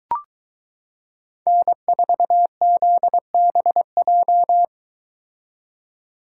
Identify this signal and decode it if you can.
Morse code, heard as a keyed tone, about 22 wpm, 700 Hz. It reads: N4ZBJ